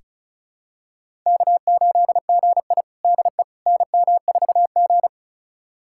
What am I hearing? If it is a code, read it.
Morse code, heard as a keyed tone, about 36 wpm, 700 Hz. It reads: K8GI DE NM4G
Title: USA Callsign DE Callsign - 4 Characters - 35wpm